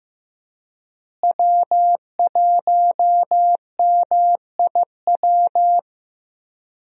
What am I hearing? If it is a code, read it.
Morse code, heard as a keyed tone, about 15 wpm, 700 Hz. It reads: W1MIW
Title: USA Call Signs - 5 Characters - Repeated 3x - 15wpm